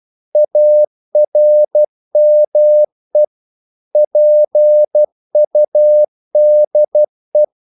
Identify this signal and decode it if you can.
Morse code, heard as a keyed tone, about 12 wpm, 600 Hz. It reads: ARME PUDE
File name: Korte ord DK 12wpm